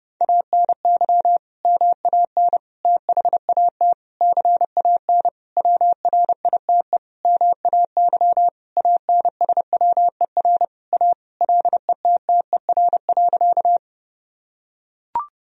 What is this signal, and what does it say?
Morse code, heard as a keyed tone, about 32 wpm, 700 Hz. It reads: ANY MAN THAT CAN WRITE MAY ANSWER A LETTER.
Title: Sentences from Top 1000 Words - Review Part 1 - 30wpm